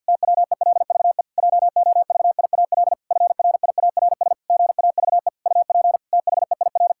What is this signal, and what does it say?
Morse code, heard as a keyed tone, about 52 wpm, 700 Hz. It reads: TWELVE JOVIAL FRIARS GAVE UP THEIR
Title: Pangram Challenge v2 - 50wpm